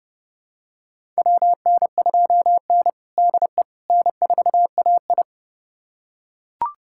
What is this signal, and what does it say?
Morse code, heard as a keyed tone, about 30 wpm, 700 Hz. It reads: WN2N DE N4AI
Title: USA Callsign DE Callsign - 4 Characters - 30wpm